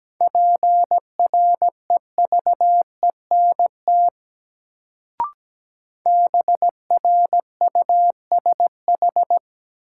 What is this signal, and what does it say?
Morse code, heard as a keyed tone, about 17 wpm, 700 Hz. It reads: PREVENT BRUSH
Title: Sentences from Top 1100 Words - New Word Drill - 17wpm